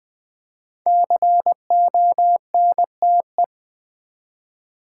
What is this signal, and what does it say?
Morse code, heard as a keyed tone, about 20 wpm, 700 Hz. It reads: CONTE
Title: Build A Word - 5 Letters At A Time - 10-Letter Words - 20wpm